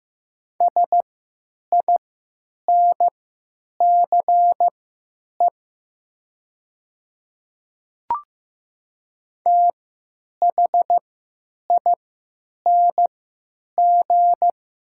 Morse code, heard as a keyed tone, about 15 wpm, spaced as for 8 wpm, 700 Hz. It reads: SINCE THING
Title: 3x Character Spacing - Top 300 Words - 15wpm